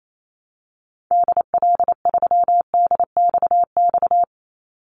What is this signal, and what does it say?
Morse code, heard as a keyed tone, about 28 wpm, 700 Hz. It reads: DL3DXX
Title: Call Signs - 28wpm